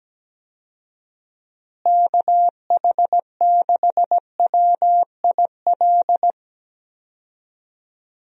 Morse code, heard as a keyed tone, about 17 wpm, 700 Hz. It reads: KH6WIL
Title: USA Call Signs - 6 Characters - Repeated 3x - 17wpm